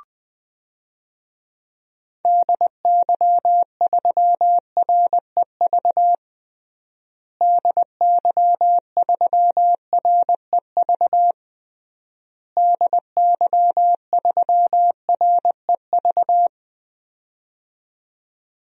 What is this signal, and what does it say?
Morse code, heard as a keyed tone, about 21 wpm, 700 Hz. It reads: DY3REV DY3REV DY3REV
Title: DXCC-100 Call Signs - 6 Characters - Repeated 3x - 20wpm